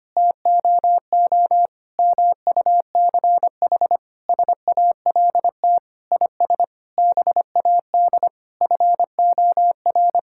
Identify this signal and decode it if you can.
Morse code, heard as a keyed tone, about 25 wpm, 700 Hz. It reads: TOO MUCH SALT IS BAD FOR
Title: Sentences from Top 1000 Words - Review Part 5 - 25wpm